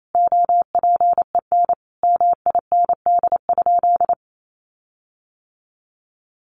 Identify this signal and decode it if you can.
Morse code, heard as a keyed tone, about 28 wpm, 700 Hz. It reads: OPEN MIND?
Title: Sentences from Top 1300 Words 28wpm